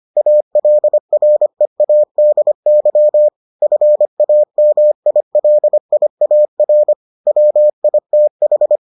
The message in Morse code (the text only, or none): ALREADY FAMILIAR WITH